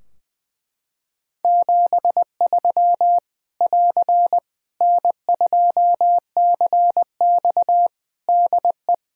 Morse code, heard as a keyed tone, about 20 wpm, 700 Hz. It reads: 73 <AR> N2CX DE
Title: Practice QSO - Exchange 4 - Ending - 20wpm